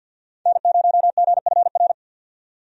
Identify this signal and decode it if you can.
Morse code, heard as a keyed tone, about 50 wpm, 700 Hz. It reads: N0GFR
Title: USA Call Signs - 5 Characters - 50wpm